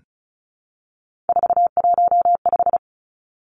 Morse code, heard as a keyed tone, about 34 wpm, 700 Hz. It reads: 415
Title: Numbers - 3-Digit 35wpm